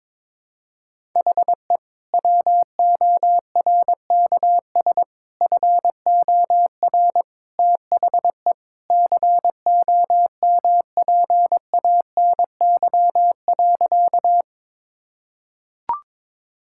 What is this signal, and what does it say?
Morse code, heard as a keyed tone, about 22 wpm, 700 Hz. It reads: HE WORKS FOR THE COMPANY.
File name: practice / 22wpm / Sentences with 2-Word Phrases - 22wpm